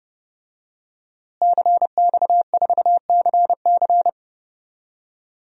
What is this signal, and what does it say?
Morse code, heard as a keyed tone, about 30 wpm, 700 Hz. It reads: CX4CC
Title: DXCC-100 Call Signs - 5 Characters - Repeated 2x - 30wpm